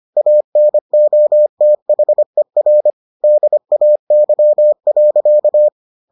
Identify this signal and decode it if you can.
Morse code, heard as a keyed tone, about 25 wpm, 600 Hz. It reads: ANOTHER DAY.